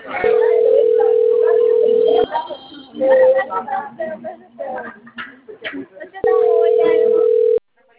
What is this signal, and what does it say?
Ringback tone (North America), 440 + 480 Hz, with speech.
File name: rg-1600-unknown- - - .56600